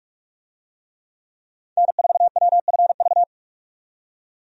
Morse code, heard as a keyed tone, about 46 wpm, 700 Hz. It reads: N4WFV
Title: USA Call Signs - 5 Characters - 45wpm